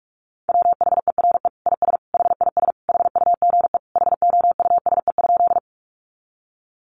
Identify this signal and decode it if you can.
Morse code, heard as a keyed tone, about 44 wpm, 700 Hz. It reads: WHERE IS HIS HUGE HOUSE?
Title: Sentences from Top 1100 Words 45wpm